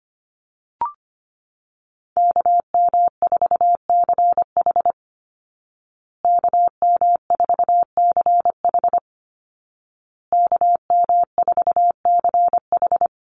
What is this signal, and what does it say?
Morse code, heard as a keyed tone, about 25 wpm, 700 Hz. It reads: KM4CH KM4CH KM4CH
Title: Call Signs Repeated 2x - 25wpm